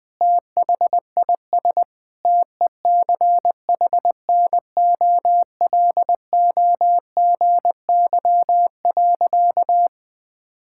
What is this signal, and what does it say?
Morse code, heard as a keyed tone, about 21 wpm, 700 Hz. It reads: THIS TECHNOLOGY.
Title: Sentences from Top 1000 Words - Review Part 5 - 20wpm